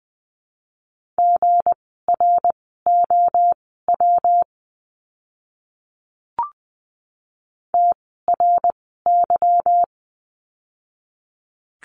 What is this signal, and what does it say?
Morse code, heard as a keyed tone, about 20 wpm, spaced as for 15 wpm, 700 Hz. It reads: GROW TRY
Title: 160 Lesson 26f - Letter g - Words 20wpm